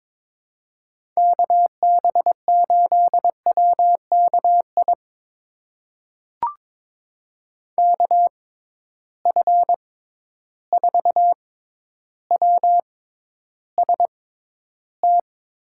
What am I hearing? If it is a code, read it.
Morse code, heard as a keyed tone, about 22 wpm, spaced as for 11 wpm, 700 Hz. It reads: KB8WKI KF4WST
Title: USA Call Signs - 6 Characters - 6x Character Spacing - 22wpm